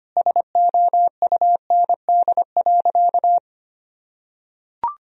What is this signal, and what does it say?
Morse code, heard as a keyed tone, about 25 wpm, 700 Hz. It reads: SOUND.
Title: Sentences from Top 1000 Words - Review Part 1 - 25wpm